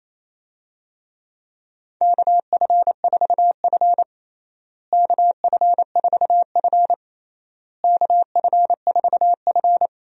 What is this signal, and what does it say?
Morse code, heard as a keyed tone, about 28 wpm, 700 Hz. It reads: KF4F KF4F KF4F
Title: USA Call Signs - 4 Characters - Repeated 3x - 28wpm